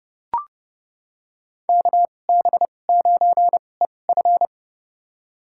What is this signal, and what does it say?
Morse code, heard as a keyed tone, about 30 wpm, spaced as for 22 wpm, 700 Hz. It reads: KB9EF
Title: USA Call Signs - 2x Character Spacing - 30wpm